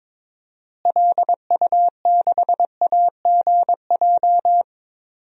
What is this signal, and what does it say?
Morse code, heard as a keyed tone, about 22 wpm, 700 Hz. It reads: LU6AGJ
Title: DXCC-100 Call Signs - 6 Characters - Repeated 3x - 22wpm